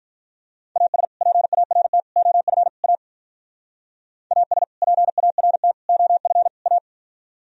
Morse code, heard as a keyed tone, about 45 wpm, 700 Hz. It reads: AS PART OF A AS PART OF A